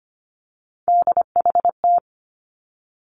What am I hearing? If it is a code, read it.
Morse code, heard as a keyed tone, about 25 wpm, 700 Hz. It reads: DHT
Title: Three Challenging Letter-Number Combinations 25wpm